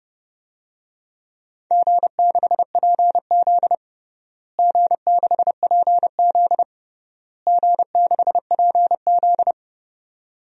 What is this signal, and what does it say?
Morse code, heard as a keyed tone, about 32 wpm, 700 Hz. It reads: G6PZ G6PZ G6PZ
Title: DXCC-100 Call Signs - 4 Characters - Repeated 3x - 30wpm